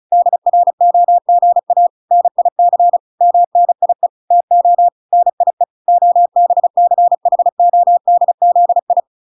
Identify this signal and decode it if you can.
Morse code, heard as a keyed tone, about 34 wpm, 700 Hz. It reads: DROGA NIC MNIE TO NIE OBCHODZI
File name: recordings / 35wpm0000